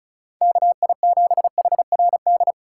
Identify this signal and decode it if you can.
Morse code, heard as a keyed tone, about 36 wpm, 700 Hz. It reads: KI7HRD
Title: USA Call Signs - 2x Character Spacing - 35wpm